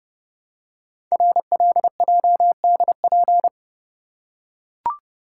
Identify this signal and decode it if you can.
Morse code, heard as a keyed tone, about 30 wpm, 700 Hz. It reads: RLJDP
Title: Code Groups - 5 Letters - 30wpm